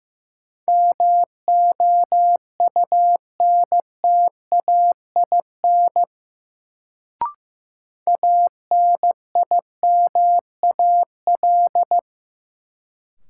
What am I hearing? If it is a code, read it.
Morse code, heard as a keyed tone, about 15 wpm, 700 Hz. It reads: MOUNTAIN ANIMAL
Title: Sentences from Top 1000 Words - Review Part 1 - New Word Drill - 15wpm